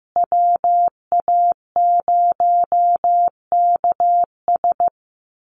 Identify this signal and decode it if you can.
Morse code, heard as a keyed tone, about 15 wpm, 700 Hz. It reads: WA0KS
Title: USA Call Signs Repeated 3x 15wpm